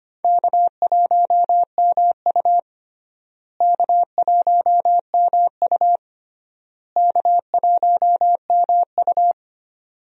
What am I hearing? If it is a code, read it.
Morse code, heard as a keyed tone, about 25 wpm, 700 Hz. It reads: K1MU K1MU K1MU
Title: USA Call Signs - 4 Characters - Repeated 3x - 25wpm